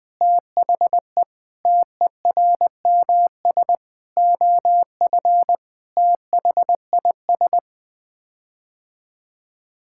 Morse code, heard as a keyed tone, about 21 wpm, 700 Hz. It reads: THE TERMS OF THIS
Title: Sets of 9 Words - 20wpm